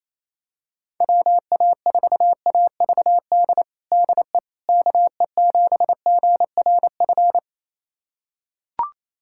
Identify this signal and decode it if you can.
Morse code, heard as a keyed tone, about 28 wpm, 700 Hz. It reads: WA4AVD DE KE7GRF
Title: USA Callsign DE Callsign 28wpm